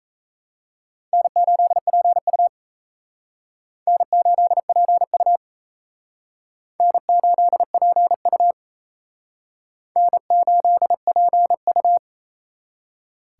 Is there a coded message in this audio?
Yes: Morse code from 1.1 to 12.0 s, heard as a keyed tone, about 35 wpm, 700 Hz. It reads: N8PU N8PU N8PU N8PU